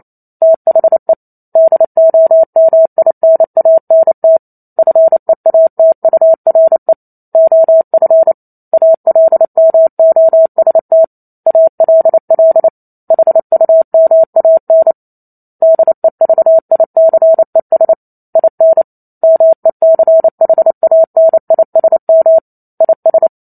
Morse code, heard as a keyed tone, about 28 wpm, 650 Hz. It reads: THE DOMINANT FEATURE OF ALMOST ALL HUMAN DEVICES IN MECHANISM IS
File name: Chapter 2 - 28 WPM